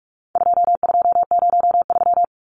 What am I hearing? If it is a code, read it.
Morse code, heard as a keyed tone, about 44 wpm, 700 Hz. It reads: 2203
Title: Numbers - 4-Digit 45wpm